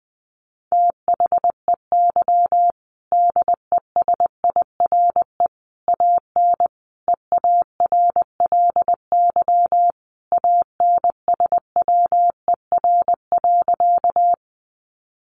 Morse code, heard as a keyed tone, about 20 wpm, 700 Hz. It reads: THEY DESIRE AN EARLY ANSWER.
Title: Sentences from Top 1100 Words 20wpm